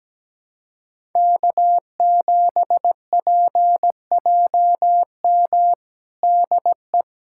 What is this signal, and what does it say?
Morse code, heard as a keyed tone, about 17 wpm, 700 Hz. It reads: K7PJM DE
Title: USA Callsign DE Callsign - 5 Characters - 17wpm